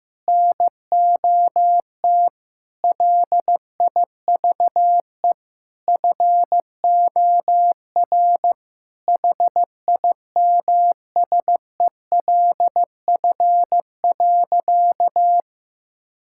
Morse code, heard as a keyed tone, about 15 wpm, 700 Hz. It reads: NOT LIVE FOR HIMSELF.
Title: Sentences from Top 1600 Words 15wpm